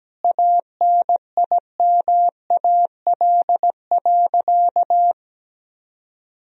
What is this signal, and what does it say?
Morse code, heard as a keyed tone, about 17 wpm, 700 Hz. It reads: ANIMAL.
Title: Sentences from Top 1000 Words - Review Part 1 - 17wpm